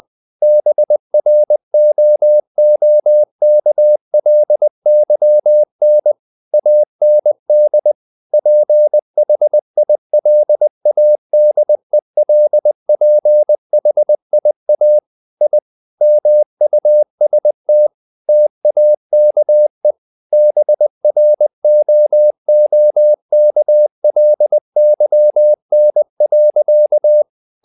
Morse code, heard as a keyed tone, about 20 wpm, 600 Hz. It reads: BROOKLYN AND PHILADELPHIA I MUST TAKE BROOKLYN.
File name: the old man and the sea / the old man and the sea part 1 20wpm